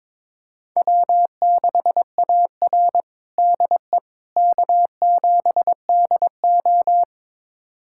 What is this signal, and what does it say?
Morse code, heard as a keyed tone, about 22 wpm, 700 Hz. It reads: W6AR DE K7DO
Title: USA Callsign DE Callsign - 4 Characters - 22wpm